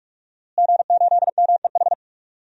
Morse code, heard as a keyed tone, about 46 wpm, 700 Hz. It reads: G8MEH
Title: DXCC-100 Call Signs - 5 Characters - Repeated 3x - 45wpm